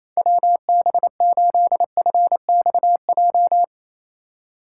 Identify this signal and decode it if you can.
Morse code, heard as a keyed tone, about 28 wpm, 700 Hz. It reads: WB8FXJ
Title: K1USN SST QSO Practice - 28wpm